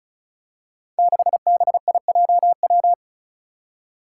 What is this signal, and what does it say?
Morse code, heard as a keyed tone, about 36 wpm, 700 Hz. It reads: 6BIJW
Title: Five Challenging Letter-Number Combinations 35wpm